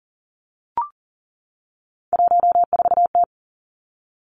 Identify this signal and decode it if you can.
Morse code, heard as a keyed tone, about 40 wpm, 690 Hz. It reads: 14T
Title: Three Challenging Letter-Number Combinations 40wpm